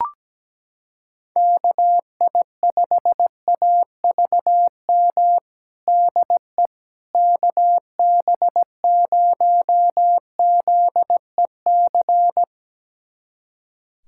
Morse code, heard as a keyed tone, about 17 wpm, 700 Hz. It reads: KI5AVM DE KB0ZEC
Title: USA Callsign DE Callsign - 6 Characters - 17wpm